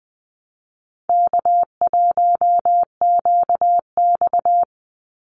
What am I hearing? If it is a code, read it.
Morse code, heard as a keyed tone, about 20 wpm, 700 Hz. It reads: K1QX
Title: Easy Call Signs Repeated 3x - 20wpm